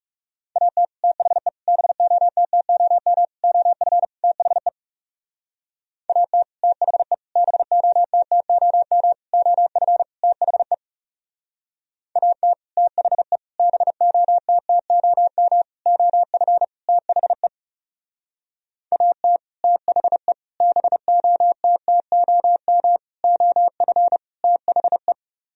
Morse code, heard as a keyed tone, about 40 wpm, 700 Hz. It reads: AT THE BOTTOM OF THE AT THE BOTTOM OF THE AT THE BOTTOM OF THE AT THE BOTTOM OF THE